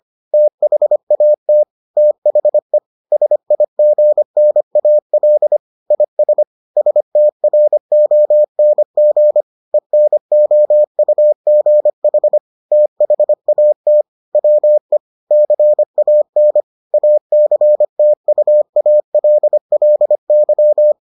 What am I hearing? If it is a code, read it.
Morse code, heard as a keyed tone, about 25 wpm, 600 Hz. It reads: THAT THE SIGNAL IS STRONG ENOUGH THAT WE CAN ACTUALLY